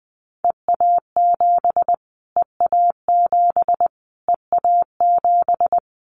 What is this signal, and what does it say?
Morse code, heard as a keyed tone, about 20 wpm, 700 Hz. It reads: EA7 EA7 EA7
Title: Callsign Prefixes for Top 25 DXCC - Repeated 3x - 20wpm